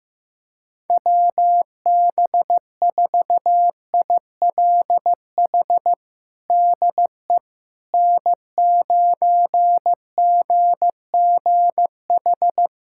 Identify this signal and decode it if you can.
Morse code, heard as a keyed tone, about 15 wpm, 700 Hz. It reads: WB4ILH DE N9GGH
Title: USA Callsign DE Callsign 15wpm